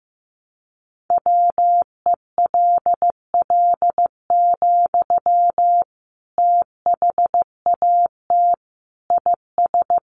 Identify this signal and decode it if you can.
Morse code, heard as a keyed tone, about 15 wpm, 700 Hz. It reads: WELL, THAT IS
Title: Sentences from Top 900 Words 15wpm